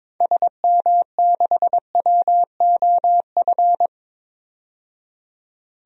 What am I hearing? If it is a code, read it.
Morse code, heard as a keyed tone, about 22 wpm, 700 Hz. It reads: SM6WOF
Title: DXCC-100 Call Signs - 6 Characters - 22wpm